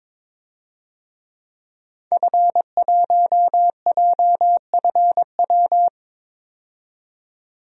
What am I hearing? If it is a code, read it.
Morse code, heard as a keyed tone, about 22 wpm, 700 Hz. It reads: F1JFW
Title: DXCC-100 Call Signs - 5 Characters - 22wpm